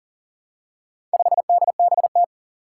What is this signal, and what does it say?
Morse code, heard as a keyed tone, about 40 wpm, 700 Hz. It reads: 5DBT
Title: Four Challenging Letter-Number Combinations 40wpm